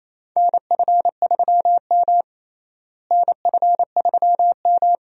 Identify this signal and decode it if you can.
Morse code, heard as a keyed tone, about 27 wpm, 700 Hz. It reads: NF3M NF3M
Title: USA Call Signs - 4 Characters - Repeated 2x - 28wpm